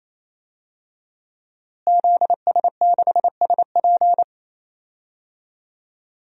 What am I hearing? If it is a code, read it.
Morse code, heard as a keyed tone, about 28 wpm, 700 Hz. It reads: ZS6SP
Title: DXCC-100 Call Signs - 5 Characters - Repeated 3x - 28wpm